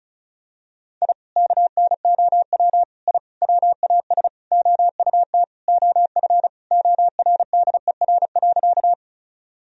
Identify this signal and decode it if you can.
Morse code, heard as a keyed tone, about 34 wpm, 700 Hz. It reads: I KNOW I WAS OUT OF ORDER.
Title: Sentences from Top 1000 Words - Review Part 2 - 35wpm